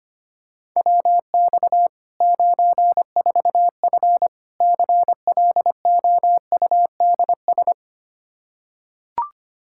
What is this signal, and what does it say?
Morse code, heard as a keyed tone, about 25 wpm, 700 Hz. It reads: WX 94F CLOUDS
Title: Practice QSO - Exchange 2 - Info - 25wpm